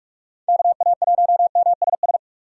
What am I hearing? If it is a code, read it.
Morse code, heard as a keyed tone, about 43 wpm, 700 Hz. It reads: KA1MSS
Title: USA Call Signs - 4x Character Spacing - 45wpm